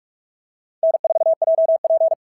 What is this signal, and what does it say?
Morse code, heard as a keyed tone, about 44 wpm, 690 Hz. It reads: N4JP
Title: SOTA QSO Practice 45wpm